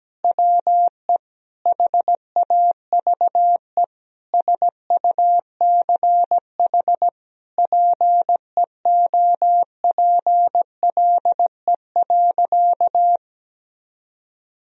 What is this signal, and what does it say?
Morse code, heard as a keyed tone, about 17 wpm, 700 Hz. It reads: WE HAVE SUCH PEOPLE.
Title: Sentences from Top 1000 Words - Review Part 3 - 17wpm